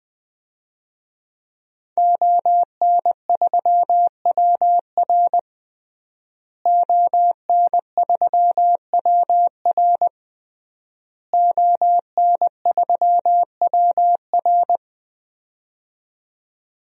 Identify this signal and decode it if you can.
Morse code, heard as a keyed tone, about 21 wpm, 700 Hz. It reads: ON3WR ON3WR ON3WR
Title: DXCC-100 Call Signs - 5 Characters - Repeated 3x - 20wpm